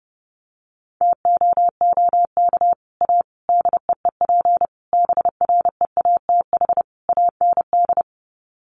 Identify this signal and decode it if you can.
Morse code, heard as a keyed tone, about 30 wpm, 700 Hz. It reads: TOOK A DEEP BREATH AND
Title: Sets of 5 Words Encore 30wpm